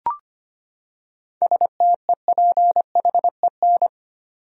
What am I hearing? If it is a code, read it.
Morse code, heard as a keyed tone, about 25 wpm, 700 Hz. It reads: STEPHEN
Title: USA Common Names - 25wpm